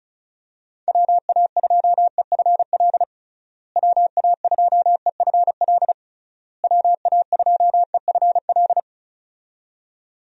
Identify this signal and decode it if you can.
Morse code, heard as a keyed tone, about 34 wpm, 700 Hz. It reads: WA2EFL WA2EFL WA2EFL
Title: USA Call Signs - 6 Characters - Repeated 3x - 35wpm